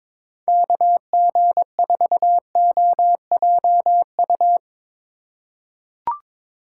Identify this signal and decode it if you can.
Morse code, heard as a keyed tone, about 22 wpm, 700 Hz. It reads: KG4OJU
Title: USA Call Signs - 6x Character Spacing - 22wpm